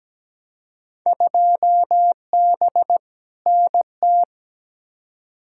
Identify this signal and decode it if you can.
Morse code, heard as a keyed tone, about 17 wpm, 700 Hz. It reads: 2B NT
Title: Field Day Exchanges Encore - 17wpm